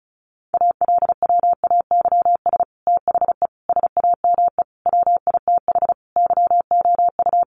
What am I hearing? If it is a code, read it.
Morse code, heard as a keyed tone, about 34 wpm, 700 Hz. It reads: ALWAYS THE SAME WITH YOU
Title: Sentences from Top 300 Words 35wpm